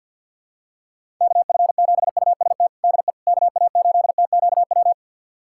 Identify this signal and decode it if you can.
Morse code, heard as a keyed tone, about 51 wpm, 700 Hz. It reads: KF7UST DE KA8TQW
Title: USA Callsign DE Callsign 50wpm